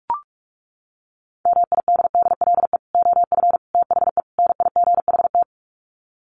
Morse code, heard as a keyed tone, about 44 wpm, 700 Hz. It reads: MIDDLE OF THE NIGHT
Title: Sets of 4 Words 45wpm